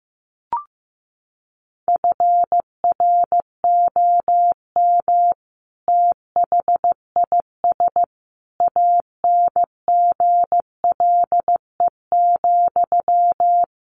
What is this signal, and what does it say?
Morse code, heard as a keyed tone, about 15 wpm, 700 Hz. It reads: FROM THIS ANGLE,
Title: Sentences from Top 1500 Words 15wpm